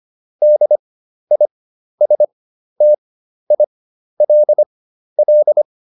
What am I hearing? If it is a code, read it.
Morse code, heard as a keyed tone, about 25 wpm, spaced as for 12 wpm, 600 Hz. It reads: DISTILL